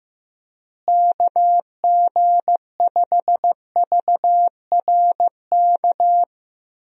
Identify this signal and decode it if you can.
Morse code, heard as a keyed tone, about 15 wpm, 700 Hz. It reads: KG5VRK
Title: USA Call Signs - 6 Characters - Repeated 2x - 15wpm